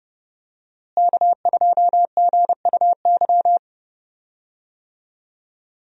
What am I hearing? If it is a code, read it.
Morse code, heard as a keyed tone, about 30 wpm, 700 Hz. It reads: K2GUY
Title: USA Call Signs - 5 Characters - Repeated 3x - 30wpm